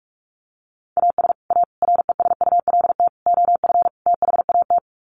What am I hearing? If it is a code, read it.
Morse code, heard as a keyed tone, about 44 wpm, 700 Hz. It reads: AS A RESULT OF THAT
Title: Sets of 5 Words Encore 45wpm